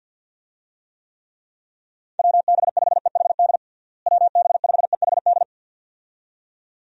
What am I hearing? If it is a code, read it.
Morse code, heard as a keyed tone, about 52 wpm, 700 Hz. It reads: WB5EHD WB5EHD
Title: USA Call Signs - 6 Characters - Repeated 2x - 50wpm